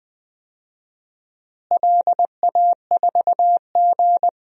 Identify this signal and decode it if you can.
Morse code, heard as a keyed tone, about 20 wpm, 700 Hz. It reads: LA4G
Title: DXCC-100 Call Signs - 4 Characters - Repeated 2x - 20wpm